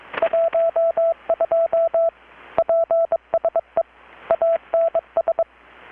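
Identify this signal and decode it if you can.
Morse code, heard as a keyed tone, about 22 wpm, 660 Hz. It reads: 12 PSE ANS
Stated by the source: Morse from KPH's Night of Nights XX maritime radiotelegraphy event, received in Chiapas, Mexico about 3,800 km away on a Kenwood R-600 with a grounded 10 m long-wire antenna.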